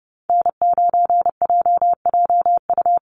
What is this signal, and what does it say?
Morse code, heard as a keyed tone, about 30 wpm, 700 Hz. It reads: N9JJU
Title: USA Call Signs Repeated 3x 30wpm